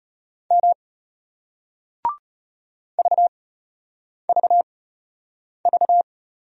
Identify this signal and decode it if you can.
Morse code, heard as a keyed tone, about 34 wpm, 690 Hz. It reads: M V V V